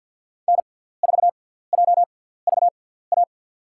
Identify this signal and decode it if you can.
Morse code, heard as a keyed tone, about 47 wpm, spaced as for 17 wpm, 700 Hz. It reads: N4JVA
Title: USA Call Signs - 5 Characters - 6x Character Spacing - 50wpm